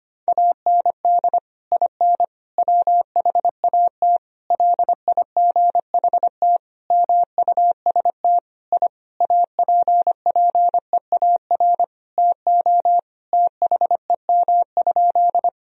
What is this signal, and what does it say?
Morse code, heard as a keyed tone, about 25 wpm, 700 Hz. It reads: AND IN WHAT LIGHT MUST I APPEAR TO THEM?
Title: Sentences from Top 1000 Words - Review Part 2 - 25wpm